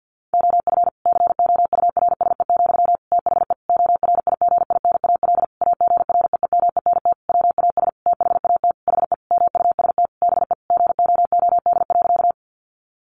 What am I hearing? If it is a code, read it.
Morse code, heard as a keyed tone, about 50 wpm, 700 Hz. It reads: OF COURSE, THE ORIGINAL AGREEMENT WAS THAT HE MUST BE GOOD.
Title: Sentences from Top 1500 Words 50wpm